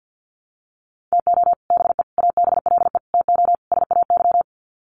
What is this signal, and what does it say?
Morse code, heard as a keyed tone, about 50 wpm, 700 Hz. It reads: TO BE ABLE TO SAY
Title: Sets of 5 Words Encore2 50wpm